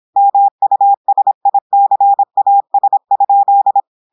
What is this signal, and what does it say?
Morse code, heard as a keyed tone, about 26 wpm, 800 Hz. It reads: MUSICAS?
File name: portugais texte en clair 2 vitesse 26 mots minutes